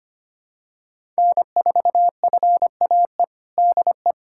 Morse code, heard as a keyed tone, about 25 wpm, 700 Hz. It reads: N4FAE DE
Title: USA Callsign DE Callsign - 5 Characters - 25wpm